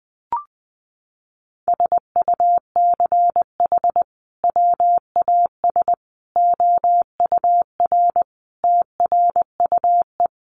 Morse code, heard as a keyed tone, about 20 wpm, 700 Hz. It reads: SUCH WAS OUR TRUE
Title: Sentences from Top 1200 Words 20wpm